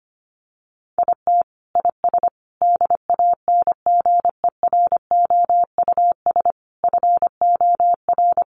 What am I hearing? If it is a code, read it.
Morse code, heard as a keyed tone, about 25 wpm, 700 Hz. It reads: IT IS DANGEROUS FOR
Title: Sentences from Top 1200 Words 25wpm